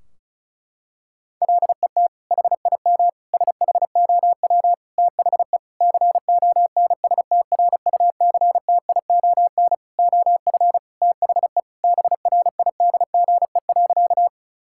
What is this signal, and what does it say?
Morse code, heard as a keyed tone, about 34 wpm, 700 Hz. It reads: LET HIM SHOW THE CONSTRUCTION OF THE BRIDGE.
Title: Sentences from Top 1600 Words 35wpm